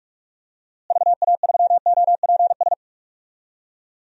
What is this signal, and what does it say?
Morse code, heard as a keyed tone, about 46 wpm, 700 Hz. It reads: VA3OPS
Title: DXCC-100 Call Signs - 6 Characters - Repeated 2x - 45wpm